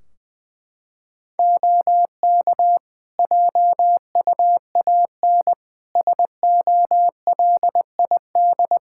Morse code, heard as a keyed tone, about 20 wpm, 700 Hz. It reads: OK JUAN SOLID
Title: Practice QSO - Exchange 3 - More Info - 20wpm